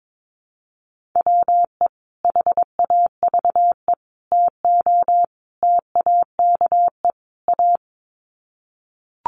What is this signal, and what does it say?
Morse code, heard as a keyed tone, about 22 wpm, 700 Hz. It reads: WE HAVE TO TAKE A
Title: Sets of 5 Words Encore 22wpm